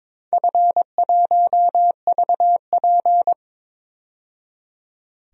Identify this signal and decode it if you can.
Morse code, heard as a keyed tone, about 23 wpm, 700 Hz. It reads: F1VP
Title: DXCC-100 Call Signs - 4 Characters - 22wpm